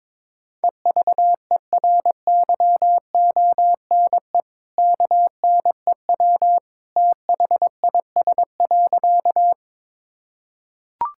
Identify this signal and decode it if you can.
Morse code, heard as a keyed tone, about 21 wpm, 700 Hz. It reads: EVERYONE KNEW THIS.
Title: Sentences from Top 1000 Words - Review Part 5 - 22wpm